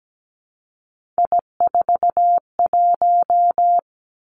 Morse code, heard as a keyed tone, about 17 wpm, 700 Hz. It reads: I41
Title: Three Challenging Letter-Number Combinations 17wpm